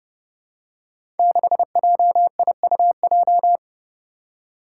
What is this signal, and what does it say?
Morse code, heard as a keyed tone, about 31 wpm, 700 Hz. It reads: 6JIUJ
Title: Five Challenging Letter-Number Combinations 30wpm